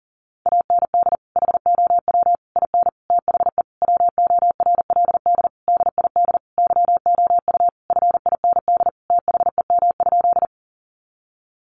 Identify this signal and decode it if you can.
Morse code, heard as a keyed tone, about 40 wpm, 700 Hz. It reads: AND HOW IN THE WORLD DID YOU FIND THEM?
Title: Sentences from Top 200 Words 40wpm